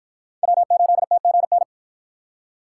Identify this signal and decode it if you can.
Morse code, heard as a keyed tone, about 53 wpm, 700 Hz. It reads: W8TGN